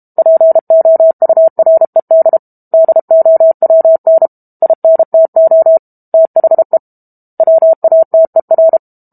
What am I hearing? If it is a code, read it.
Morse code, heard as a keyed tone, about 32 wpm, 650 Hz. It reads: POURED DOWN INTO THE WATER
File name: Book Two / 06-Chapter 6 - 32 WPM